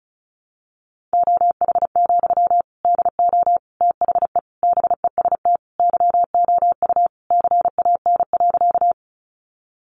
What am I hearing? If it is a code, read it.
Morse code, heard as a keyed tone, about 34 wpm, 700 Hz. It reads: OH, DO THE BEST YOU CAN.
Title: Sentences from Top 500 Words 35wpm